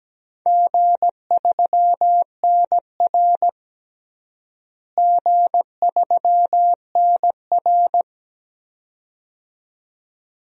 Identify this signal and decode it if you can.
Morse code, heard as a keyed tone, about 17 wpm, 700 Hz. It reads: G3NR G3NR
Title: DXCC-100 Call Signs - 4 Characters - Repeated 2x - 17wpm